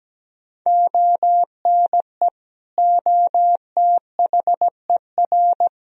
Morse code, heard as a keyed tone, about 17 wpm, 700 Hz. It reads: ONE OTHER
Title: Sentences from Top 400 Words and Abbreviations 17wpm